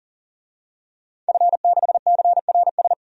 Morse code, heard as a keyed tone, about 40 wpm, 700 Hz. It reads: F6CRS
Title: DXCC-100 Call Signs - 5 Characters - 40wpm